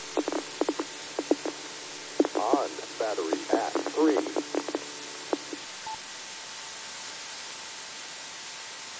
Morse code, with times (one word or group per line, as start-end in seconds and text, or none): none